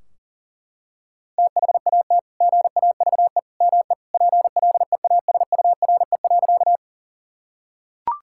Morse code, heard as a keyed tone, about 37 wpm, 700 Hz. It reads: THAT GAVE ME PLEASURE.
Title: Sentences from Top 1600 Words 40wpm